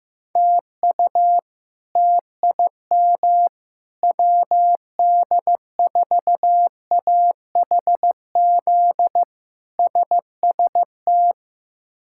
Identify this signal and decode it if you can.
Morse code, heard as a keyed tone, about 15 wpm, 700 Hz. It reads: TU TIM WD4AHZ SST
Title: K1USN SST QSO Practice - 15wpm